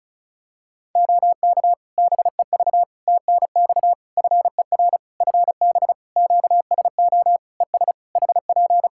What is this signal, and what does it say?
Morse code, heard as a keyed tone, about 36 wpm, 700 Hz. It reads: OK BEV TNX FER FB QSO ES HP
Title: Practice QSO - Exchange 4 - Ending - 35wpm